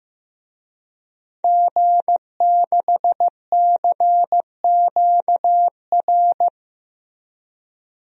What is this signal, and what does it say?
Morse code, heard as a keyed tone, about 15 wpm, 700 Hz. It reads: G6CQR
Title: DXCC-100 Call Signs - 5 Characters - 15wpm